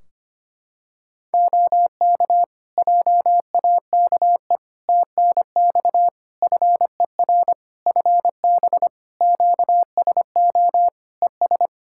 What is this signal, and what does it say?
Morse code, heard as a keyed tone, about 25 wpm, 700 Hz. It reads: OK JAKE TNX FER FB QSO ES
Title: Practice QSO - Exchange 4 - Ending - 25wpm